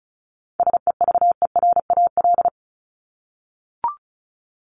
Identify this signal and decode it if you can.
Morse code, heard as a keyed tone, about 34 wpm, 700 Hz. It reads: SEVERAL
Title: Top 500 Words 35wpm